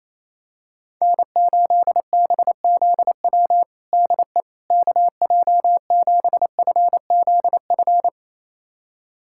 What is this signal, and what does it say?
Morse code, heard as a keyed tone, about 28 wpm, 700 Hz. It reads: N8BZW DE KJ7FZF
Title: USA Callsign DE Callsign 28wpm